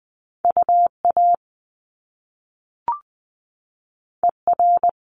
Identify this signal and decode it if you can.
Morse code, heard as a keyed tone, about 20 wpm, 700 Hz. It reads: UA ER
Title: Top Two Letter Combinations in Words 20wpm